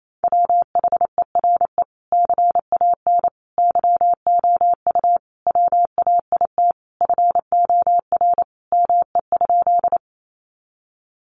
Morse code, heard as a keyed tone, about 28 wpm, 700 Hz. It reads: WHERE CAN YOU WAIT FOR ME?
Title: Sentences from Top 500 Words 28wpm